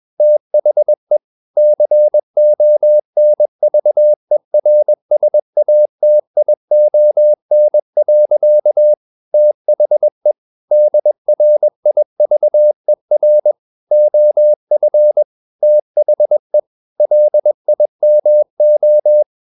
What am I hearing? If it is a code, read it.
Morse code, heard as a keyed tone, about 21 wpm, 600 Hz. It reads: THE CONVERSATION. THE DRIVER OF THE LIMO
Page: CW med Gnister